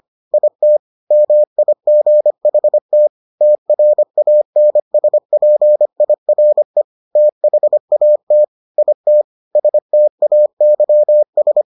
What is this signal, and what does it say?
Morse code, heard as a keyed tone, about 25 wpm, 600 Hz. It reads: IT MIGHT TRANSPIRE THAT IT STAYS